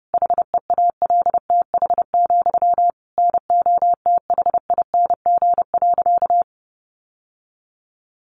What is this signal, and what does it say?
Morse code, heard as a keyed tone, about 30 wpm, 700 Hz. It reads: HEALTH, NOTHING.
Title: Sentences from Top 900 Words 30wpm